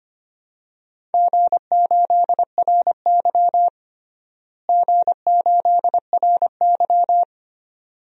Morse code, heard as a keyed tone, about 25 wpm, 700 Hz. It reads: G8RY G8RY
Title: DXCC-100 Call Signs - 4 Characters - Repeated 2x - 25wpm